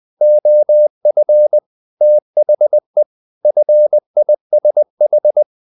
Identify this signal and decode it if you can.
Morse code, heard as a keyed tone, about 20 wpm, 600 Hz. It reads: OF THE FISH
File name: the old man and the sea part 2 20wpm